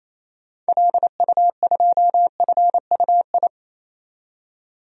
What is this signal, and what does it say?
Morse code, heard as a keyed tone, about 27 wpm, 700 Hz. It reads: LU2FUI
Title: DXCC-100 Call Signs - 6 Characters - 28wpm